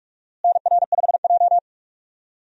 Morse code, heard as a keyed tone, about 41 wpm, 700 Hz. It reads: NR5J
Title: USA Call Signs - 4 Characters - Repeated 2x - 45wpm